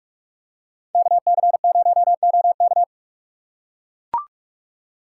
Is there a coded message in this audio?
Yes: Morse code from 0.9 to 2.8 s, heard as a keyed tone, about 44 wpm, 700 Hz. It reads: KC0OK